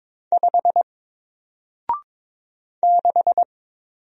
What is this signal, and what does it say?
Morse code, heard as a keyed tone, about 22 wpm, 700 Hz. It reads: 5 6
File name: Single-Letter-Number-22wpm